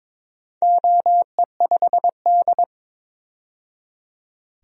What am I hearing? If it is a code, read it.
Morse code, heard as a keyed tone, about 22 wpm, 700 Hz. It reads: OE5D
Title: DXCC-100 Call Signs - 4 Characters - 22wpm